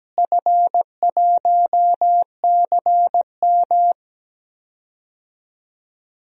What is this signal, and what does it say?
Morse code, heard as a keyed tone, about 17 wpm, 700 Hz. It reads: F1CM
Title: DXCC-100 Call Signs - 4 Characters - Repeated 2x - 17wpm